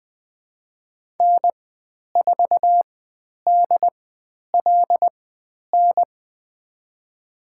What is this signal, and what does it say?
Morse code, heard as a keyed tone, about 20 wpm, spaced as for 10 wpm, 700 Hz. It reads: N4DLN
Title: 095 Lesson 16g - Letter c - Call Signs 20wpm